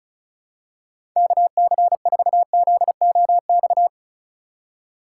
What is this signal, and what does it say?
Morse code, heard as a keyed tone, about 36 wpm, 700 Hz. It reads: KC4ZOX
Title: USA Call Signs - 6 Characters - 4x Character Spacing - 35wpm